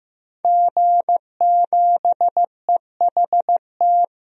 Morse code, heard as a keyed tone, about 15 wpm, 700 Hz. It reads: G7EHT
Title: DXCC-100 Call Signs - 5 Characters - Repeated 2x - 15wpm